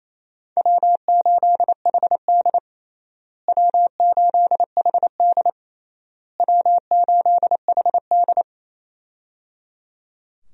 Morse code, heard as a keyed tone, about 28 wpm, 700 Hz. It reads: W8HD W8HD W8HD
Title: USA Call Signs - 4 Characters - Repeated 3x - 28wpm